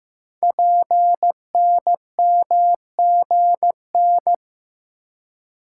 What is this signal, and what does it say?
Morse code, heard as a keyed tone, about 15 wpm, 700 Hz. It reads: PNMGN
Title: Code Groups - 5 Letters - 15wpm